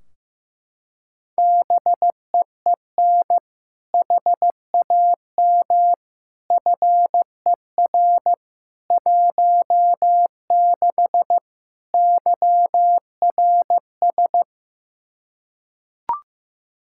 Morse code, heard as a keyed tone, about 15 wpm, 700 Hz. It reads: BEEN HAM FER 16 YRS
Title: Practice QSO - Exchange 3 - More Info - 15wpm